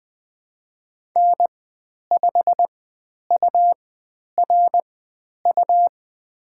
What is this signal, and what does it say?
Morse code, heard as a keyed tone, about 20 wpm, spaced as for 10 wpm, 700 Hz. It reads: N5URU
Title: 102 Lesson 17g - Letter u - Call Signs 20wpm